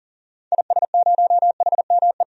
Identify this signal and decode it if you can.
Morse code, heard as a keyed tone, about 40 wpm, 700 Hz. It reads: IS0HME
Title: DXCC-100 Call Signs - 6 Characters - 40wpm